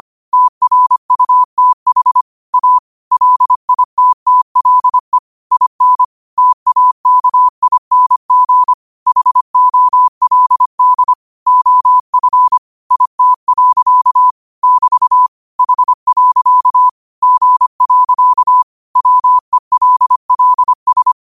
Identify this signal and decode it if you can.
Morse code, heard as a keyed tone, about 25 wpm, 1000 Hz. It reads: TRUTH A LITTLE IN TAKING HOLD OF IT. <BT> H. G. WELLS
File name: qotdfiles / 25wpm